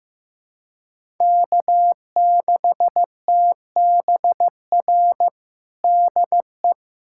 Morse code, heard as a keyed tone, about 15 wpm, 700 Hz. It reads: K6TBR DE
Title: USA Callsign DE Callsign - 5 Characters - 15wpm